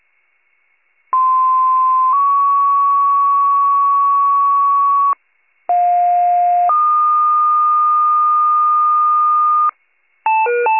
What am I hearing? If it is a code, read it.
Two-tone page, heard as a one-second tone, then a three-second tone, about 1040 then 1120 Hz, about 710 then 1220 Hz. It is heard from 1.1 to 9.7 s.